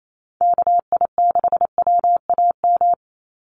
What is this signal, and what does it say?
Morse code, heard as a keyed tone, about 28 wpm, 700 Hz. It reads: KI6WAM
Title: USA Call Signs 28wpm